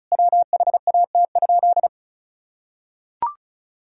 Morse code, heard as a keyed tone, about 34 wpm, 700 Hz. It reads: WHAT?
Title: Sentences from Top 1000 Words - Review Part 4 - 35wpm